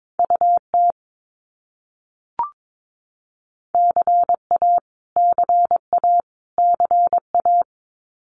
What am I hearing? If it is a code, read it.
Morse code, heard as a keyed tone, about 22 wpm, 700 Hz. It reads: UT CA CA CA
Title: US States Repeated 3x 22wpm